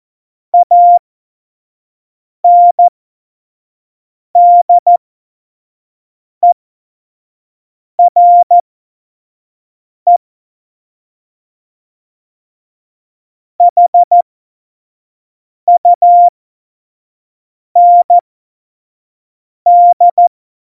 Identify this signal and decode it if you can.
Morse code, heard as a keyed tone, about 14 wpm, spaced as for 5 wpm, 700 Hz. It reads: ANDERE HUND
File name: Morsekurs / Witze / Witz 18. Dezember 2018, 25 F70